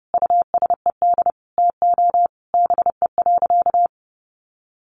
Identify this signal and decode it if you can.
Morse code, heard as a keyed tone, about 30 wpm, 700 Hz. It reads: USED TO BE.
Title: Sentences from Top 1100 Words 30wpm